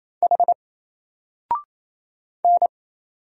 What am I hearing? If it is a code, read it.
Morse code, heard as a keyed tone, about 27 wpm, 700 Hz. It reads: H N
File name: Single-Letter-Number-28wpm